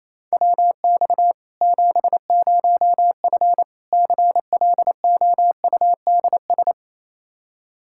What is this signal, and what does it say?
Morse code, heard as a keyed tone, about 28 wpm, 700 Hz. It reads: WX 70F CLOUDS
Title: K1USN SST QSO Practice 28wpm